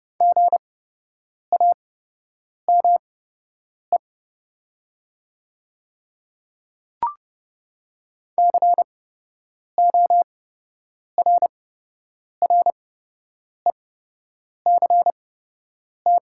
Morse code, heard as a keyed tone, about 30 wpm, spaced as for 8 wpm, 700 Hz. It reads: GAME CORRECT
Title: 8x Character Spacing - Top 500 Words - 30wpm